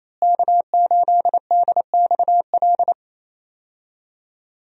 Morse code, heard as a keyed tone, about 28 wpm, 700 Hz. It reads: K8DXL
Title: USA Call Signs - 5 Characters - Repeated 3x - 28wpm